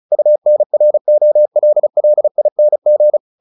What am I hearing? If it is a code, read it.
Morse code, heard as a keyed tone, about 34 wpm, 600 Hz. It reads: UNROLLING
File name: the old man and the sea / the old man and the sea 1 35wpm